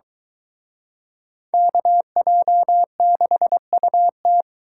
Morse code, heard as a keyed tone, about 23 wpm, 700 Hz. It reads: KJ6UT